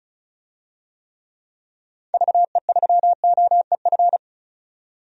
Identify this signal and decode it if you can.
Morse code, heard as a keyed tone, about 36 wpm, 700 Hz. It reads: VE3OEF
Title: DXCC-100 Call Signs - 6 Characters - 35wpm